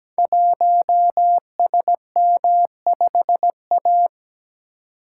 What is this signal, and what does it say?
Morse code, heard as a keyed tone, about 17 wpm, 700 Hz. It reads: 1SM5A
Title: Code Groups - 5 Letter:Numbers - 17wpm